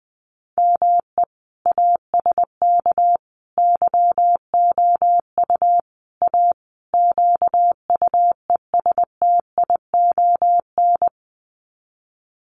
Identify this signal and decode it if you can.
Morse code, heard as a keyed tone, about 20 wpm, 700 Hz. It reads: ME ASK YOU A QUESTION
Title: Sets of 5 Words Encore2 20wpm